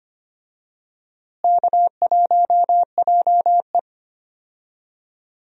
Morse code, heard as a keyed tone, about 25 wpm, 700 Hz. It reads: K1JE
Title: USA Call Signs - 4 Characters - Repeated 3x - 25wpm